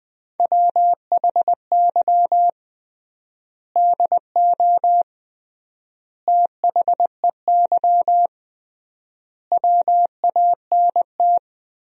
Morse code, heard as a keyed tone, about 20 wpm, 700 Hz. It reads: WHY DO THEY WANT
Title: 3x Word-Spacing - Sentences from Top 400 Words - 20wpm